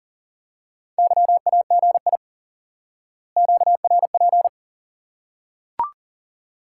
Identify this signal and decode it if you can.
Morse code, heard as a keyed tone, about 37 wpm, 700 Hz. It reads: YAGI QRP
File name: IQE Course 3 - Lesson 89 - YAGI - Two Elements - 40wpm